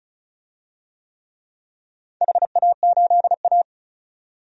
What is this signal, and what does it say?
Morse code, heard as a keyed tone, about 36 wpm, 700 Hz. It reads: HA8A
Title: DXCC-100 Call Signs - 4 Characters - 35wpm